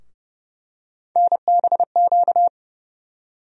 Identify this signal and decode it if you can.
Morse code, heard as a keyed tone, about 30 wpm, 700 Hz. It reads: NBQ
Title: USA Call Sign Postfixes - 3 Letters 30wpm